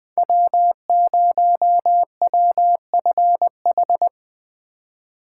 Morse code, heard as a keyed tone, about 19 wpm, 700 Hz. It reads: W0WFH
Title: Speed Racing - USA Call Signs 17wpm